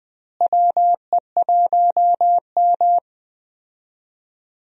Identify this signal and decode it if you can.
Morse code, heard as a keyed tone, about 20 wpm, 700 Hz. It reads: WE1M
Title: USA Call Signs - 4 Characters - Repeated 3x - 20wpm